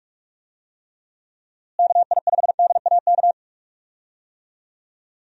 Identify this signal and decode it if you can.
Morse code, heard as a keyed tone, about 46 wpm, 700 Hz. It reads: KI5DAK
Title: USA Call Signs - 6 Characters - Repeated 3x - 45wpm